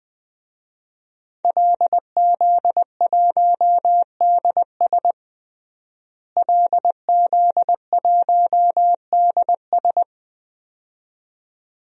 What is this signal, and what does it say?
Morse code, heard as a keyed tone, about 20 wpm, 700 Hz. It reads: LZ1DS LZ1DS
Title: DXCC-100 Call Signs - 5 Characters - Repeated 2x - 20wpm